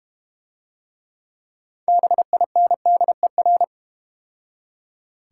Morse code, heard as a keyed tone, about 32 wpm, 700 Hz. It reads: BINDER